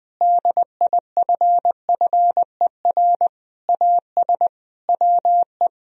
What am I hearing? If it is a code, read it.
Morse code, heard as a keyed tone, about 21 wpm, 700 Hz. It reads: DIFFER AS WE
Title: Sentences from Top 1000 Words - Review Part 1 - 20wpm